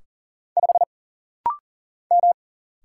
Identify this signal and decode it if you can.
Morse code, heard as a keyed tone, about 40 wpm, 700 Hz. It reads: 5 M
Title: Single Letter-Number Alt - Rapid-Fire 40wpm